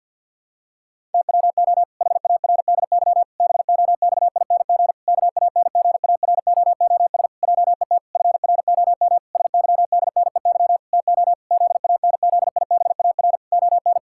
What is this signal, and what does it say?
Morse code, heard as a keyed tone, about 48 wpm, 700 Hz. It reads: TWO HARDY BOXING KANGAROOS JET FROM SYDNEY TO ZANZIBAR ON
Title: Pangram Challenge v2 - 50wpm